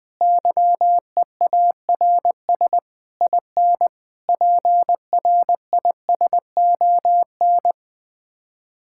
Morse code, heard as a keyed tone, about 20 wpm, 700 Hz. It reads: YEARS IN PRISON
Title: Sets of 6 Words - 20wpm